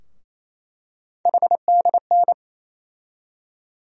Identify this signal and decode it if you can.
Morse code, heard as a keyed tone, about 28 wpm, 700 Hz. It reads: HDN
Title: USA Call Sign Postfixes - 3 Letters 28wpm